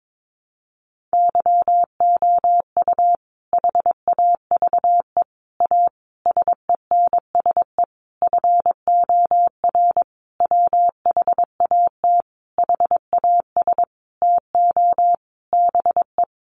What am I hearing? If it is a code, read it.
Morse code, heard as a keyed tone, about 22 wpm, 700 Hz. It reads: YOU HAVE A SENSE FOR WHAT HAS TO BE
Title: Sentences from Top 700 Words 22wpm